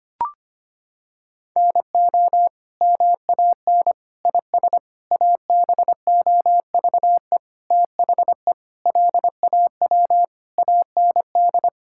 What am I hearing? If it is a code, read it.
Morse code, heard as a keyed tone, about 25 wpm, 700 Hz. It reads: NO MAN IS ABOVE THE LAW AND
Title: Sentences from Top 1000 Words - Review Part 5 - 25wpm